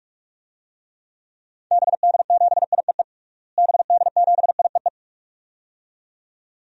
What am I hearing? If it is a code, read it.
Morse code, heard as a keyed tone, about 46 wpm, 700 Hz. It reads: BD7IEE BD7IEE
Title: DXCC-100 Call Signs - 6 Characters - Repeated 2x - 45wpm